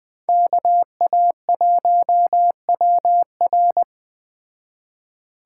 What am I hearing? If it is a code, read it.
Morse code, heard as a keyed tone, about 20 wpm, 700 Hz. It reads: KA1WR